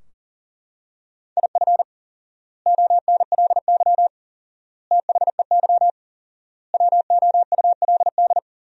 Morse code, heard as a keyed tone, about 40 wpm, 700 Hz. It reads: IF ONLY THEY WOULD
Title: 4x Word-Spacing - Sentences from Top 500 Words - 40wpm